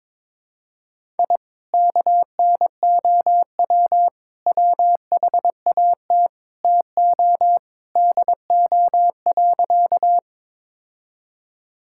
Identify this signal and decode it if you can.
Morse code, heard as a keyed tone, about 22 wpm, 700 Hz. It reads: I KNOW WHAT TO DO.
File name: practice / 22wpm / Sentences with 3-Word Phrases - 22wpm